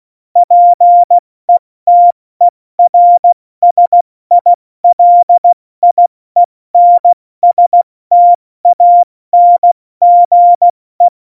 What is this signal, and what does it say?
Morse code, heard as a keyed tone, about 16 wpm, spaced as for 14 wpm, 700 Hz. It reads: PETERSILIENSTANGE
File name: Morsekurs / Witze / Witz 16. April 2019, 70 F80